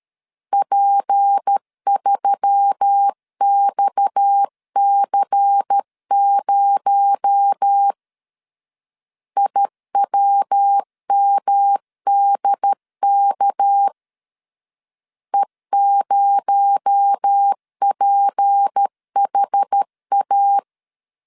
Morse code, heard as a keyed tone, about 13 wpm, 800 Hz. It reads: P3XC0 IWMDK E0PHA
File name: LEZIONI / 6 PKX0 / 6 PKX0 12wpm 2231